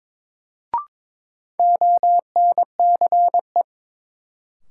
Morse code, heard as a keyed tone, about 22 wpm, 700 Hz. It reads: ONCE
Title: Sentences from Top 300 Words - New Word Drill - 22wpm